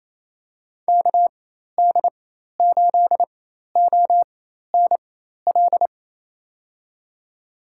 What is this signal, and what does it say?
Morse code, heard as a keyed tone, about 28 wpm, spaced as for 13 wpm, 700 Hz. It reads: KD8ONL
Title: USA Call Signs - 6 Characters - 4x Character Spacing - 28wpm